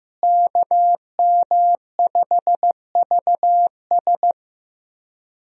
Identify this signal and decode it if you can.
Morse code, heard as a keyed tone, about 15 wpm, 700 Hz. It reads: KM5VS
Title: USA Call Signs - 5 Characters - Repeated 3x - 15wpm